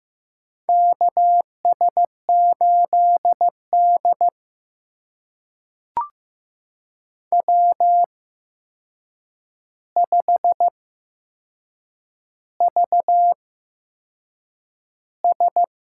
Morse code, heard as a keyed tone, about 15 wpm, spaced as for 6 wpm, 700 Hz. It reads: KS8D W5VS
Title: USA Call Signs - 4 Characters - 8x Character Spacing - 15wpm